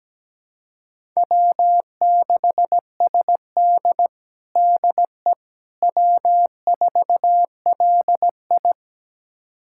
Morse code, heard as a keyed tone, about 17 wpm, 700 Hz. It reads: W6SD DE W4LI
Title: USA Callsign DE Callsign - 4 Characters - 17wpm